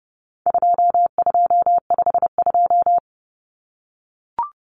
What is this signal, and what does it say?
Morse code, heard as a keyed tone, about 30 wpm, 700 Hz. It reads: 2252
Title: Common Cut Numbers - 4-Digit 30wpm